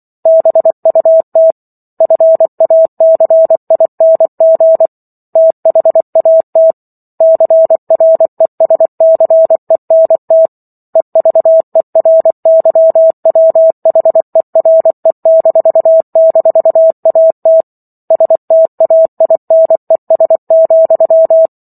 Morse code, heard as a keyed tone, about 24 wpm, 650 Hz. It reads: BUT FACING THAT CRESCENT EVERYWHERE--AT STAINES,